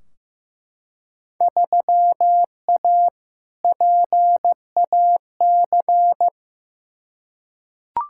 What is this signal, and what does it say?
Morse code, heard as a keyed tone, about 15 wpm, 700 Hz. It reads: 3A PAC
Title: Field Day Exchanges Encore - 15wpm